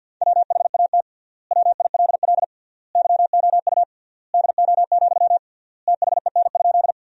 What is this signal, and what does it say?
Morse code, heard as a keyed tone, about 50 wpm, 700 Hz. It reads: WHAT WILL YOU DO, THEN?
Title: 3x Word-Spacing - Sentences from Top 100 Words - 50wpm